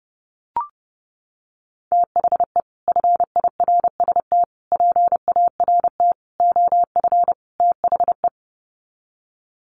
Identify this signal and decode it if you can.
Morse code, heard as a keyed tone, about 30 wpm, 700 Hz. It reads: THE FIRST PART OF THE
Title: Sets of 5 Words Encore 30wpm